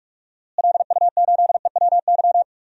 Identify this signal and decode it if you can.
Morse code, heard as a keyed tone, about 44 wpm, 700 Hz. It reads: LU8EWY
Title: DXCC-100 Call Signs - 6 Characters - Repeated 2x - 45wpm